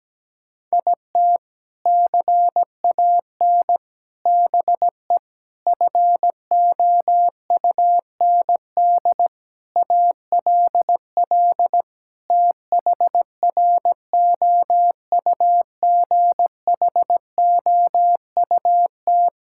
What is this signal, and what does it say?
Morse code, heard as a keyed tone, about 17 wpm, 700 Hz. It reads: IT CAN BE FOUND ALL THROUGHOUT
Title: Sentences from Top 1000 Words - Review Part 5 - 17wpm